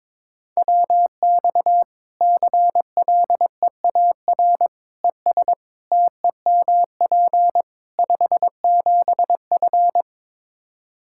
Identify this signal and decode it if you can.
Morse code, heard as a keyed tone, about 22 wpm, 700 Hz. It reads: WX CLEAR ES TEMP 57F
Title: Practice QSO - Exchange 2 - Info - 22wpm